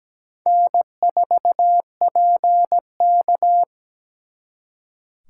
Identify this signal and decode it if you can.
Morse code, heard as a keyed tone, about 21 wpm, 700 Hz. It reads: N4PK
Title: Speed Racing - USA 1x2 Call Signs 17wpm